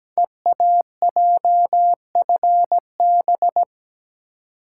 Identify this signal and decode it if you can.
Morse code, heard as a keyed tone, about 17 wpm, 700 Hz. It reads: EAJFB
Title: Code Groups - 5 Letter:Numbers - 17wpm